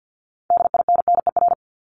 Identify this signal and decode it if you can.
Morse code, heard as a keyed tone, about 50 wpm, 700 Hz. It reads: DINNER
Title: Multi-letter Phonemes Repeated 3x 50wpm